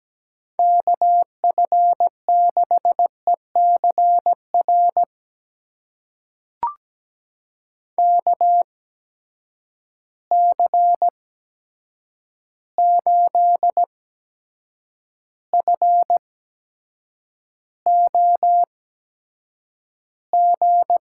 Morse code, heard as a keyed tone, about 17 wpm, spaced as for 7 wpm, 700 Hz. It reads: KF6ECR KC8FOG
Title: USA Call Signs - 6 Characters - 8x Character Spacing - 17wpm